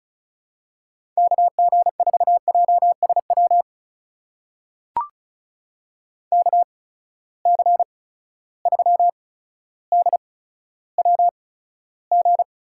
Morse code, heard as a keyed tone, about 36 wpm, spaced as for 15 wpm, 700 Hz. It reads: KG4JSW KC3DWG
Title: USA Call Signs - 6 Characters - 8x Character Spacing - 35wpm